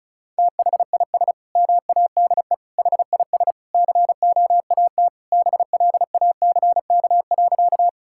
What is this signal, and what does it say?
Morse code, heard as a keyed tone, about 36 wpm, 700 Hz. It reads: THIS MADE HIS COAT BLACK.
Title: Sentences from Top 1600 Words 35wpm